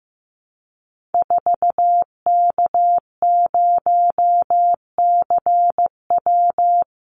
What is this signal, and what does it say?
Morse code, heard as a keyed tone, about 15 wpm, 700 Hz. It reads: 4K0CW
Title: Call Signs Repeated 3x - 15wpm